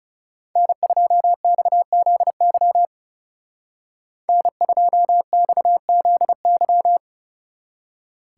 Morse code, heard as a keyed tone, about 36 wpm, 700 Hz. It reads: N2XZY N2XZY
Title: Speed Racing - USA 1x3 Call Signs 30wpm